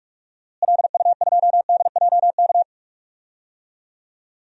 Morse code, heard as a keyed tone, about 46 wpm, 700 Hz. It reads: LU2DJK
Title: DXCC-100 Call Signs - 6 Characters - Repeated 3x - 45wpm